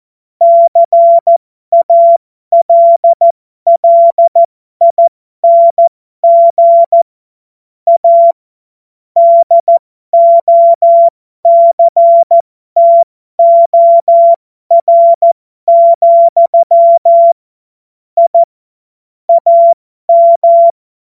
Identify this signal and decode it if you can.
Morse code, heard as a keyed tone, about 14 wpm, spaced as for 12 wpm, 680 Hz. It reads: CALLING A DOCTOR, I AM